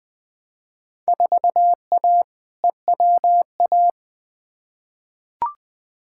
Morse code, heard as a keyed tone, about 20 wpm, 700 Hz. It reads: 4A EWA
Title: Field Day Exchanges - 20wpm